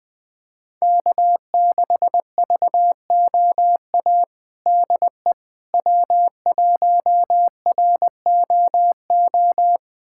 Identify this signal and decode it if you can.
Morse code, heard as a keyed tone, about 21 wpm, 700 Hz. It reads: K6VOA DE W1ROO
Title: USA Callsign DE Callsign - 5 Characters - 20wpm